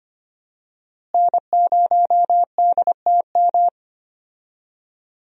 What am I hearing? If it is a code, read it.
Morse code, heard as a keyed tone, about 25 wpm, 700 Hz. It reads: N0DTM
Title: USA Call Signs - 5 Characters - Repeated 3x - 25wpm